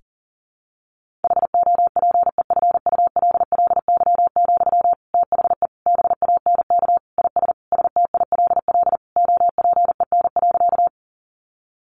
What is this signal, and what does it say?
Morse code, heard as a keyed tone, about 40 wpm, 700 Hz. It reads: HOPEFULLY, THE BANK IS STILL OPEN.
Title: Sentences from Top 1500 Words 40wpm